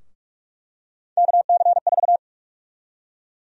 Morse code, heard as a keyed tone, about 46 wpm, 700 Hz. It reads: KC4
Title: USA Call Sign Prefixes - Repeated 2x - 45wpm